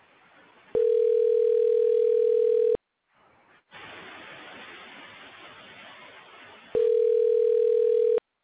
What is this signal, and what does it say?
Ringback tone (North America), 440 + 480 Hz.